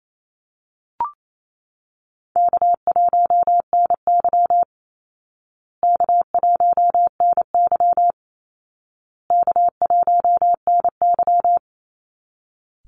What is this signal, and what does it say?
Morse code, heard as a keyed tone, about 28 wpm, 700 Hz. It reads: K1NY K1NY K1NY
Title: Easy Call Signs Repeated 3x - 28wpm